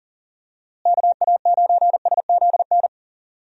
Morse code, heard as a keyed tone, about 40 wpm, 700 Hz. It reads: KA9SZN
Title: USA Call Signs - 6 Characters - Repeated 2x - 40wpm